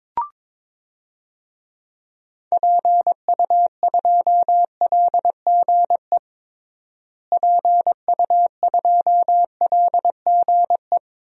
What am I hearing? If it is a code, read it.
Morse code, heard as a keyed tone, about 22 wpm, 700 Hz. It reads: PU2LGE PU2LGE
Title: DXCC-100 Call Signs - 6 Characters - Repeated 2x - 22wpm